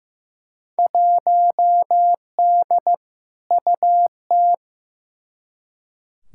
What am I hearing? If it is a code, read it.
Morse code, heard as a keyed tone, about 16 wpm, 700 Hz. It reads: 1D UT
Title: Field Day Exchanges - 15wpm